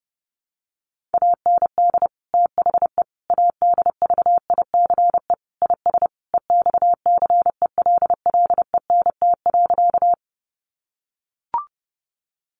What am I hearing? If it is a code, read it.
Morse code, heard as a keyed tone, about 30 wpm, 700 Hz. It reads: AND THE ADVICE IS EXCELLENT.
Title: Sentences from Top 1100 Words 30wpm